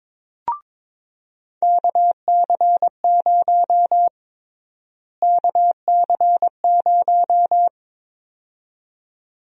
Morse code, heard as a keyed tone, about 22 wpm, 700 Hz. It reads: KC0 KC0
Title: USA Call Sign Prefixes - Repeated 2x - 22wpm